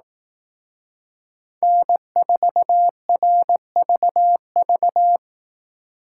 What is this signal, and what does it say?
Morse code, heard as a keyed tone, about 19 wpm, 700 Hz. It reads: N4RVV
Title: Speed Racing - USA 1x3 Call Signs 15wpm